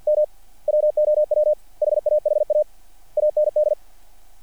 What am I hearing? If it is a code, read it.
Morse code, heard as a keyed tone, about 48 wpm, 600 Hz. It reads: M WOW HAHA AND
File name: archived / CRSnet